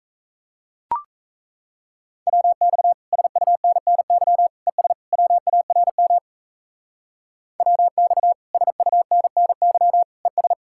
Morse code, heard as a keyed tone, about 35 wpm, 700 Hz. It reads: WX SUNNY ES WARM WX SUNNY ES
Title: Speed-Racing Common QSO Phrases - 28wpm